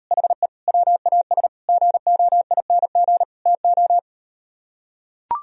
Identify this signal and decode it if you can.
Morse code, heard as a keyed tone, about 38 wpm, 700 Hz. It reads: HE WAS GOING TO